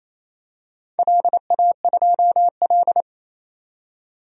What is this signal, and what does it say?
Morse code, heard as a keyed tone, about 27 wpm, 700 Hz. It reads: LA2L
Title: DXCC-100 Call Signs - 4 Characters - 28wpm